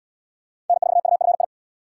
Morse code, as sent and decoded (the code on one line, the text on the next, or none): -. ....- ..- ..-. ..
N4UFI